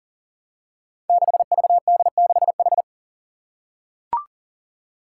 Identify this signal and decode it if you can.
Morse code, heard as a keyed tone, about 40 wpm, 700 Hz. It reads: 6VD6H
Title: Five Challenging Letter-Number Combinations 40wpm